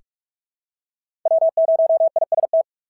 Morse code, heard as a keyed tone, about 44 wpm, 650 Hz. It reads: W0IST
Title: SOTA QSO Practice 45wpm